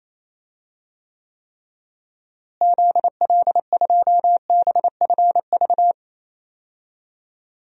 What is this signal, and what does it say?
Morse code, heard as a keyed tone, about 28 wpm, 700 Hz. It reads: ZL2BFV
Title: DXCC-100 Call Signs - 6 Characters - Repeated 3x - 28wpm